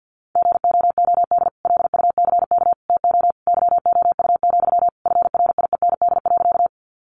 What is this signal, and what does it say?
Morse code, heard as a keyed tone, about 50 wpm, 700 Hz. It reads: GOOD LUCK TO YOU, FRIEND.
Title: Sentences from Top 1500 Words 50wpm